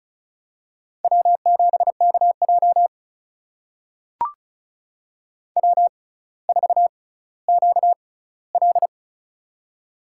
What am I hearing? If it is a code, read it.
Morse code, heard as a keyed tone, about 34 wpm, spaced as for 18 wpm, 700 Hz. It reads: W7KJ W4QL
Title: USA Call Signs - 6x Character Spacing - 35wpm